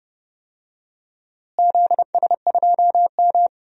Morse code, heard as a keyed tone, about 30 wpm, 700 Hz. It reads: ZS2M
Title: DXCC-100 Call Signs - 4 Characters - Repeated 3x - 30wpm